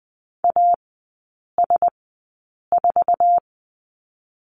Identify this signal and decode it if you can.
Morse code, heard as a keyed tone, about 20 wpm, spaced as for 8 wpm, 700 Hz. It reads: AS4
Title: 067 Lesson 12e - Letter d - Three Characters 20wpm